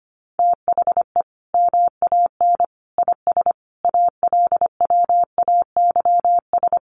Morse code, heard as a keyed tone, about 25 wpm, 700 Hz. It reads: THE MAN IS ALWAYS
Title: Sentences from Top 1400 Words 25wpm